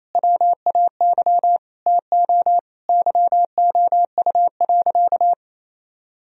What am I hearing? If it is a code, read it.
Morse code, heard as a keyed tone, about 28 wpm, 700 Hz. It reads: WAY TO YOU.
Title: Sentences from Top 1000 Words - Review Part 1 - 28wpm